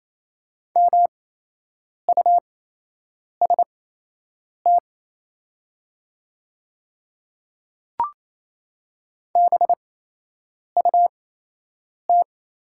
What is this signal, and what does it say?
Morse code, heard as a keyed tone, about 28 wpm, spaced as for 8 wpm, 700 Hz. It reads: MUST BUT
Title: 8x Character Spacing - Top 300 Words - 28wpm